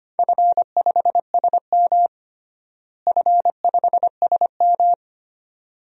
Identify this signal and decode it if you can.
Morse code, heard as a keyed tone, about 25 wpm, 700 Hz. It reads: F5SM F5SM
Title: DXCC-100 Call Signs - 4 Characters - Repeated 2x - 25wpm